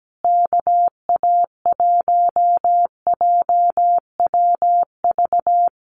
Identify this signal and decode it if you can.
Morse code, heard as a keyed tone, about 17 wpm, 700 Hz. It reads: KA1JWV
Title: Common QSO Phrases Encore 17wpm